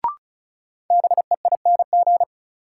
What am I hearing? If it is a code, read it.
Morse code, heard as a keyed tone, about 34 wpm, 700 Hz. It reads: BEING
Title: Sets of 4 Words - New Word Drill - 35wpm